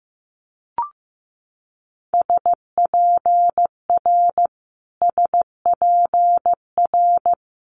Morse code, heard as a keyed tone, about 15 wpm, 700 Hz. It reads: SPR SPR
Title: Multi-letter Phonemes Repeated 3x 15wpm